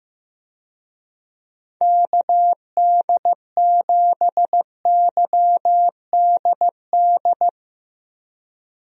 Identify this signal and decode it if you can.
Morse code, heard as a keyed tone, about 16 wpm, 700 Hz. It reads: KD7YDD
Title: USA Call Signs - 6 Characters - Repeated 2x - 15wpm